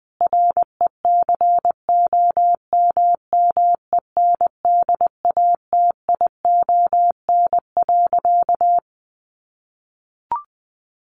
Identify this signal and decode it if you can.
Morse code, heard as a keyed tone, about 20 wpm, 700 Hz. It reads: RECOMMENDATION.
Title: Sentences from Top 1400 Words 20wpm